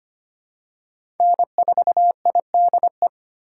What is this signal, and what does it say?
Morse code, heard as a keyed tone, about 25 wpm, 700 Hz. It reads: N4IDE
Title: USA Call Signs - 5 Characters - 4x Character Spacing - 25wpm